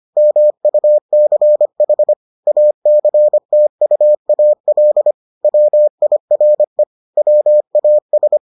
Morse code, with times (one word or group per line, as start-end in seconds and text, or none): 0.2-2.1 MUCH
2.5-5.1 ACTUAL
5.4-6.8 WIRE
7.2-8.4 WAS